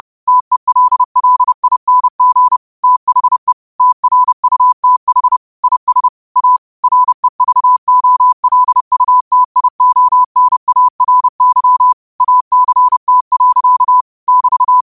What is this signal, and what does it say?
Morse code, heard as a keyed tone, about 30 wpm, 1000 Hz. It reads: TELLING THE TRUTH IS A REVOLUTIONARY ACT. <BT>
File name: qotdfiles / 30wpm